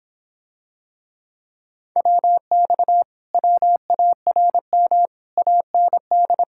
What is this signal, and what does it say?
Morse code, heard as a keyed tone, about 26 wpm, 700 Hz. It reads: WX WARM AND